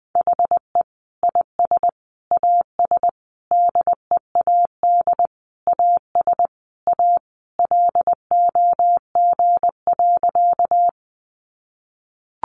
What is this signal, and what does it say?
Morse code, heard as a keyed tone, about 19 wpm, 700 Hz. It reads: HE IS AS DEAD AS A LOG.
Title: Sentences from Top 1500 Words 20wpm